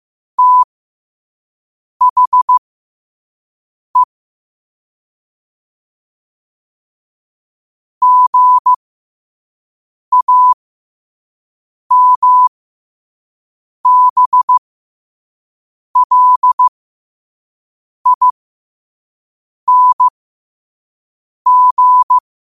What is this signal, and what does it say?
Morse code, heard as a keyed tone, about 15 wpm, spaced as for 5 wpm, 1000 Hz. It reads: THE GAMBLING